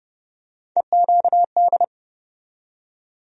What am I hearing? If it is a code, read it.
Morse code, heard as a keyed tone, about 31 wpm, 700 Hz. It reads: EQD
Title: USA Call Sign Postfixes - 3 Letters 30wpm